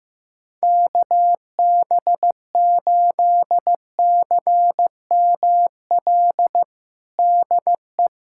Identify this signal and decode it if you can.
Morse code, heard as a keyed tone, about 15 wpm, 700 Hz. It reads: KB8CML DE
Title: USA Callsign DE Callsign 15wpm